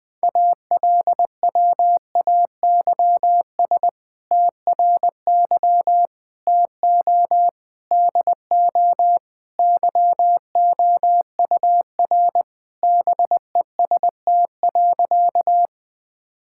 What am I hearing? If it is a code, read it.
Morse code, heard as a keyed tone, about 20 wpm, 700 Hz. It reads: ALWAYS TRY TO DO YOUR BEST.
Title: Sentences with 2-Word Phrases - 20wpm